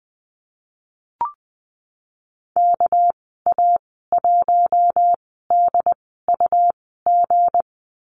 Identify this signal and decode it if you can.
Morse code, heard as a keyed tone, about 20 wpm, spaced as for 15 wpm, 700 Hz. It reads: KA1DUG
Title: 219 Lesson 35g - Number 0 - Call Signs 20wpm